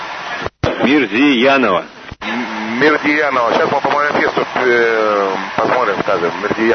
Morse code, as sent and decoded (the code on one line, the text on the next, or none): none